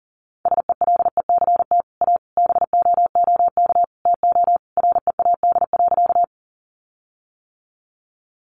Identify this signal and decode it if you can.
Morse code, heard as a keyed tone, about 40 wpm, 700 Hz. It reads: SELECT A BOOK TO READ.
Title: Sentences from Top 1300 Words 40wpm